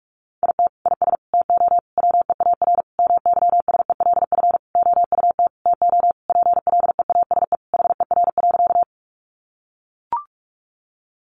Morse code, heard as a keyed tone, about 44 wpm, 700 Hz. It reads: IT IS TO WEAR MYSELF OUT TO PLEASE HER.
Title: Sentences from Top 1500 Words 45wpm